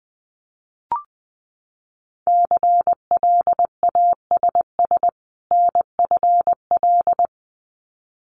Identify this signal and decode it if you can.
Morse code, heard as a keyed tone, about 19 wpm, 700 Hz. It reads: CLASS NFL
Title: Winter Field Day Exchanges 20wpm